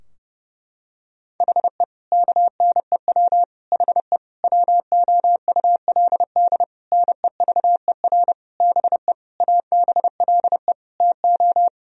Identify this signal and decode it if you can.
Morse code, heard as a keyed tone, about 32 wpm, 700 Hz. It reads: HE KNEW HE WOULD NEVER BE ABLE TO
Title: Sentences from Top 1000 Words - Review Part 2 - 30wpm